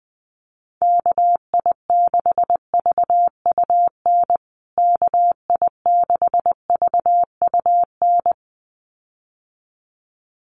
Morse code, heard as a keyed tone, about 19 wpm, 700 Hz. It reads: KI6VUN KI6VUN
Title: USA Call Signs Repeated 2x 20wpm